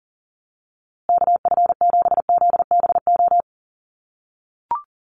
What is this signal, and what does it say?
Morse code, heard as a keyed tone, about 40 wpm, 700 Hz. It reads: KF7ZBO
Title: USA Call Signs Repeated 3x 40wpm